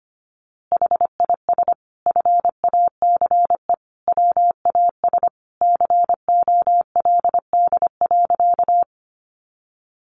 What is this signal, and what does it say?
Morse code, heard as a keyed tone, about 25 wpm, 700 Hz. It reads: HIS FACE WAS COLD.
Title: A Sound Of Thunder 25wpm